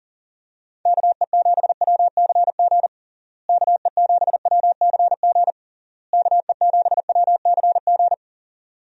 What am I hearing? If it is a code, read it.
Morse code, heard as a keyed tone, about 40 wpm, 700 Hz. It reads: KE7WCG KE7WCG KE7WCG
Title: USA Call Signs - 6 Characters - Repeated 3x - 40wpm